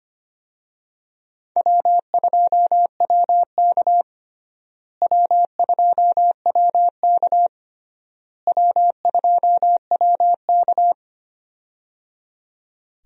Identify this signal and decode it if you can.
Morse code, heard as a keyed tone, about 25 wpm, 700 Hz. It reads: W2WK W2WK W2WK
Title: USA Call Signs - 4 Characters - Repeated 3x - 25wpm